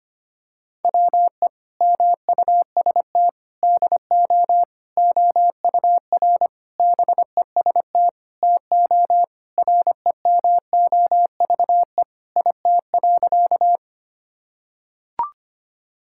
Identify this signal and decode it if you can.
Morse code, heard as a keyed tone, about 25 wpm, 700 Hz. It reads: WE MUST DO OUR BEST TO REMOVE IT.
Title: Sentences from Top 1000 Words - Review Part 5 - 25wpm